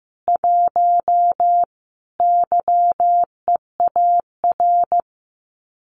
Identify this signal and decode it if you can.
Morse code, heard as a keyed tone, about 14 wpm, 700 Hz. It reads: 1 YEAR
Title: Common QSO Phrases Encore 15wpm